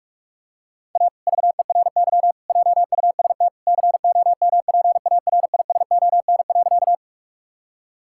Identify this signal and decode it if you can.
Morse code, heard as a keyed tone, about 46 wpm, 700 Hz. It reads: A VERY JUST COMPARISON.
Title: Sentences from Top 1600 Words 45wpm